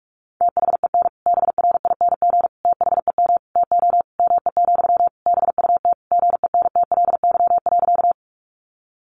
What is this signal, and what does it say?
Morse code, heard as a keyed tone, about 44 wpm, 700 Hz. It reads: THEN BRING THEM TO ME, BUT GENTLY.
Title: Sentences from Top 1500 Words 45wpm